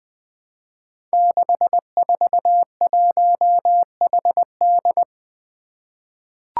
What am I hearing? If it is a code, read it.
Morse code, heard as a keyed tone, about 20 wpm, 700 Hz. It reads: 641HD
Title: Five Challenging Letter-Number Combinations 20wpm